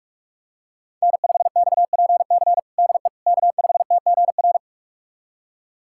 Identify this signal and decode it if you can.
Morse code, heard as a keyed tone, about 46 wpm, 700 Hz. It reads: N5XPC DE K5TGR
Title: USA Callsign DE Callsign - 5 Characters - 45wpm